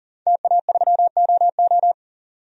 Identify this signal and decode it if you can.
Morse code, heard as a keyed tone, about 40 wpm, 700 Hz. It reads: TA3OO
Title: DXCC-100 Call Signs - 5 Characters - 40wpm